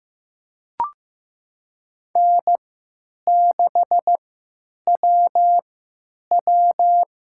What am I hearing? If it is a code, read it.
Morse code, heard as a keyed tone, about 15 wpm, spaced as for 9 wpm, 700 Hz. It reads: N6WW
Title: USA Call Signs - 3x Character Spacing - 15wpm